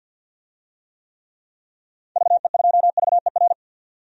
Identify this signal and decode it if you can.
Morse code, heard as a keyed tone, about 51 wpm, 700 Hz. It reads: VE2VER
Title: DXCC-100 Call Signs - 6 Characters - 50wpm